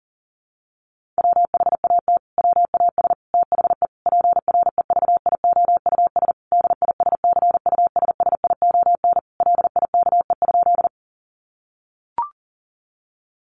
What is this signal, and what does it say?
Morse code, heard as a keyed tone, about 40 wpm, 700 Hz. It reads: WHAT WAS THE PREVIOUS DISCUSSION LIKE?
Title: Sentences from Top 1500 Words 40wpm